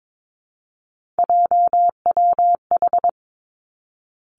Morse code, heard as a keyed tone, about 22 wpm, 700 Hz. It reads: JWH
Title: Three Challenging Letter-Number Combinations 22wpm